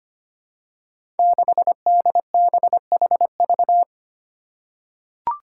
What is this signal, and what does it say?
Morse code, heard as a keyed tone, about 25 wpm, 700 Hz. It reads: 6DBHV
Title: Five Challenging Letter-Number Combinations 25wpm